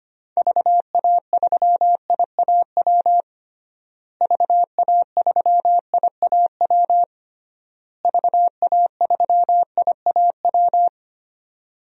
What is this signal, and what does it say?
Morse code, heard as a keyed tone, about 25 wpm, 700 Hz. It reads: VA3IAW VA3IAW VA3IAW
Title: DXCC-100 Call Signs - 6 Characters - Repeated 3x - 25wpm